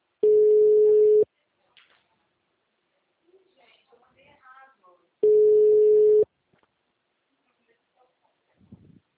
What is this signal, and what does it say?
Ringback tone (Europe, 425 Hz), with speech.
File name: out- -2008- - - .14198